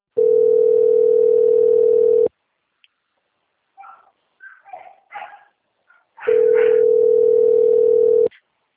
Ringback tone (North America), 440 + 480 Hz.